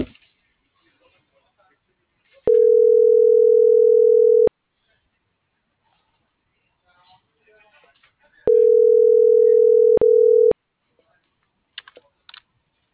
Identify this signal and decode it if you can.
Ringback tone (North America), 440 + 480 Hz.